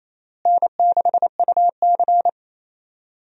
Morse code, heard as a keyed tone, about 28 wpm, 700 Hz. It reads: N6UC
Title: USA Call Signs - 4 Characters - Repeated 3x - 28wpm